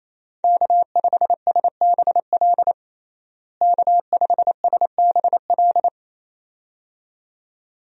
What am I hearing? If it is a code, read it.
Morse code, heard as a keyed tone, about 27 wpm, 700 Hz. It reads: K5SBL K5SBL
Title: USA Call Signs - 5 Characters - Repeated 2x - 28wpm